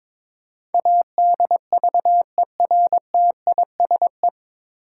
Morse code, heard as a keyed tone, about 22 wpm, 700 Hz. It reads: ADVERTISE
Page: Morse Code Ninja practice 22wpm